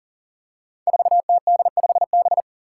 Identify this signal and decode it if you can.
Morse code, heard as a keyed tone, about 38 wpm, 700 Hz. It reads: 4TD5B
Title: Five Challenging Letter-Number Combinations 40wpm